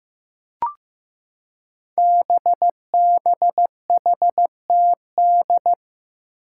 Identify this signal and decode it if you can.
Morse code, heard as a keyed tone, about 15 wpm, 700 Hz. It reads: BBHTD
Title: Five Challenging Letter-Number Combinations 15wpm